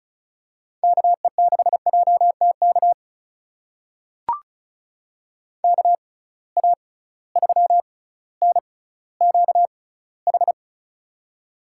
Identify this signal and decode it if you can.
Morse code, heard as a keyed tone, about 34 wpm, spaced as for 18 wpm, 700 Hz. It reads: KE6JTK KA3NQH
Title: USA Call Signs - 6 Characters - 6x Character Spacing - 35wpm